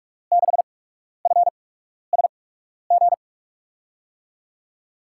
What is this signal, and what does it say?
Morse code, heard as a keyed tone, about 41 wpm, spaced as for 12 wpm, 700 Hz. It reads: 6FSG
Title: USA Call Signs - 6 Characters - 8x Character Spacing - 45wpm